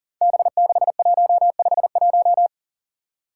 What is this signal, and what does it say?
Morse code, heard as a keyed tone, about 37 wpm, 700 Hz. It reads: B6151
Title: Five Challenging Letter-Number Combinations 40wpm